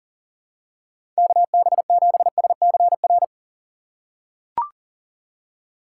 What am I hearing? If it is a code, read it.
Morse code, heard as a keyed tone, about 40 wpm, 700 Hz. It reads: KB7SCR
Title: USA Call Signs - 6x Character Spacing - 40wpm